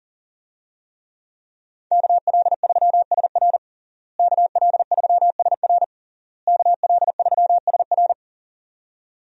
Morse code, heard as a keyed tone, about 40 wpm, 700 Hz. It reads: KL3SR KL3SR KL3SR
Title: USA Call Signs - 5 Characters - Repeated 3x - 40wpm